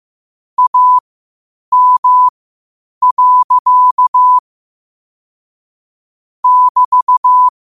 Morse code, heard as a keyed tone, about 15 wpm, spaced as for 8 wpm, 1000 Hz. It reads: AM. <BT>
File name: qotdfiles / 8wpm